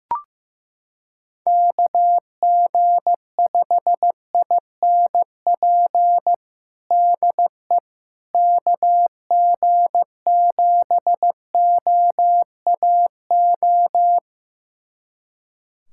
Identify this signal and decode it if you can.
Morse code, heard as a keyed tone, about 15 wpm, 700 Hz. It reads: KG5INP DE KG7OAO
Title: USA Callsign DE Callsign - 6 Characters - 15wpm